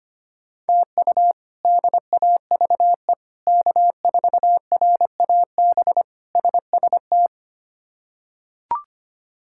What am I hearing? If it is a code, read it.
Morse code, heard as a keyed tone, about 25 wpm, 700 Hz. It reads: TU DAVE K4RAB SST
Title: K1USN SST QSO Practice - 25wpm